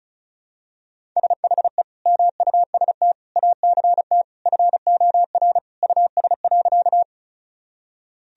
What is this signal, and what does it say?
Morse code, heard as a keyed tone, about 34 wpm, 700 Hz. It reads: SHE MUST ACT FOR US.
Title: Sentences from Top 1000 Words - Review Part 1 - 35wpm